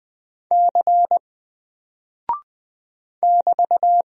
Morse code, heard as a keyed tone, about 20 wpm, 700 Hz. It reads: C <BT>
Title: Single Character (Letter:Number:Punctuation:Prosign) Alt 20wpm